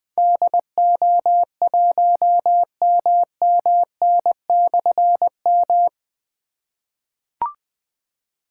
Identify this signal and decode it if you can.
Morse code, heard as a keyed tone, about 20 wpm, 700 Hz. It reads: DO1MMN/M
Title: Difficult Call Signs 20wpm